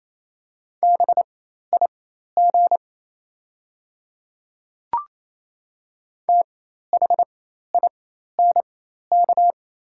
Morse code, heard as a keyed tone, about 28 wpm, spaced as for 13 wpm, 700 Hz. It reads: BIG THINK
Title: 4x Character Spacing - Top 500 Words - 28wpm